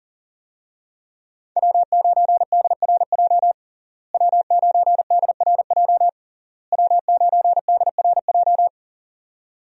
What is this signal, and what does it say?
Morse code, heard as a keyed tone, about 40 wpm, 700 Hz. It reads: W9DRJ W9DRJ W9DRJ
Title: USA Call Signs - 5 Characters - Repeated 3x - 40wpm